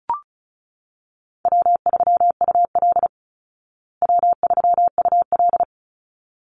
Morse code, heard as a keyed tone, about 34 wpm, 700 Hz. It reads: W3UL W3UL
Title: Easy Call Signs Repeated 2x - 35wpm